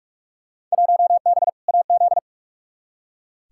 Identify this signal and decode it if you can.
Morse code, heard as a keyed tone, about 43 wpm, 700 Hz. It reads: 1B AZ
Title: Field Day Exchanges Encore - 45wpm